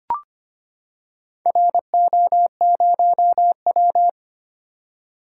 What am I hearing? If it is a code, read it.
Morse code, heard as a keyed tone, about 25 wpm, 700 Hz. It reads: RO0W
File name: practice / 25wpm / Four Letter-Number Combinations - 25wpm